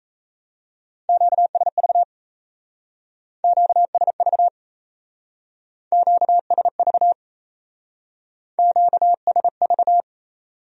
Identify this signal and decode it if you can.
Morse code, heard as a keyed tone, about 34 wpm, 700 Hz. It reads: QSV QSV QSV QSV
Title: Speed-Racing Q-Codes - 28wpm